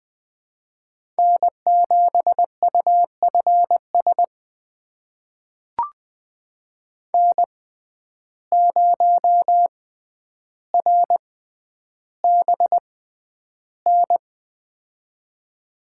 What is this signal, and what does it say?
Morse code, heard as a keyed tone, about 20 wpm, spaced as for 10 wpm, 700 Hz. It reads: N7UFS N0RBN
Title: USA Call Signs - 5 Characters - 6x Character Spacing - 20wpm